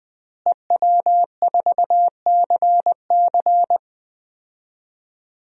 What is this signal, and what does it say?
Morse code, heard as a keyed tone, about 20 wpm, 700 Hz. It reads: EW4CC
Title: DXCC-100 Call Signs - 5 Characters - Repeated 3x - 20wpm